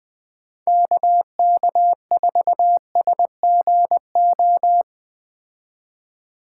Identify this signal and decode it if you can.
Morse code, heard as a keyed tone, about 19 wpm, 700 Hz. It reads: KK4SGO
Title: Speed Racing - USA 2x3 Call Signs 17wpm